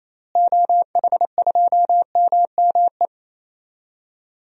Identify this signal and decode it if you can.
Morse code, heard as a keyed tone, about 28 wpm, 700 Hz. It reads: OH2MME
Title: DXCC-100 Call Signs - 6 Characters - Repeated 3x - 28wpm